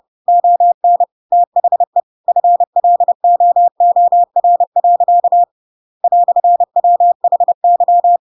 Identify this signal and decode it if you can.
Morse code, heard as a keyed tone, about 30 wpm, 700 Hz. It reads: ON THE FLOOR. "WHY
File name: Pinocchio / Pinocchio ch7 30 30 ik1hge